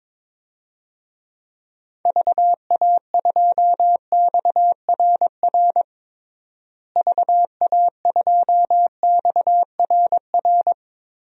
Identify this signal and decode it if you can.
Morse code, heard as a keyed tone, about 22 wpm, 700 Hz. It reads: VA2XRR VA2XRR
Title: DXCC-100 Call Signs - 6 Characters - Repeated 3x - 22wpm